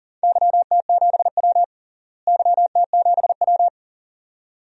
Morse code, heard as a keyed tone, about 38 wpm, 700 Hz. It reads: YT7W YT7W
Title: DXCC-100 Call Signs - 4 Characters - Repeated 2x - 40wpm